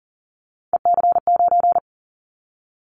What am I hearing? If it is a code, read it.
Morse code, heard as a keyed tone, about 40 wpm, 700 Hz. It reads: EC9
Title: Callsign Prefixes for Top 100 DXCC - 40wpm